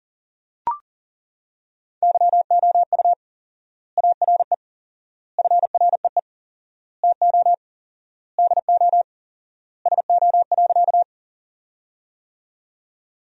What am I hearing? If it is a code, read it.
Morse code, heard as a keyed tone, about 37 wpm, 700 Hz. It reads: YOU ARE FREE TO DO SO.
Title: 4x Word-Spacing - Sentences from Top 500 Words - 40wpm